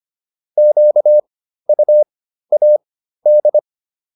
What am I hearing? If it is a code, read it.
Morse code, heard as a keyed tone, about 25 wpm, spaced as for 13 wpm, 600 Hz. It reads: QUAD